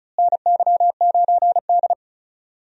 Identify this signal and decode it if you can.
Morse code, heard as a keyed tone, about 36 wpm, 700 Hz. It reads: NY9D
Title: USA Call Signs - 4 Characters - Repeated 2x - 35wpm